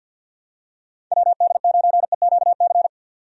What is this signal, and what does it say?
Morse code, heard as a keyed tone, about 53 wpm, 700 Hz. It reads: WD9EQC
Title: USA Call Signs - 6 Characters - 50wpm